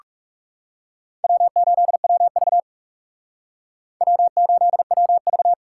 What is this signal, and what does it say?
Morse code, heard as a keyed tone, about 38 wpm, 700 Hz. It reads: W8WV W8WV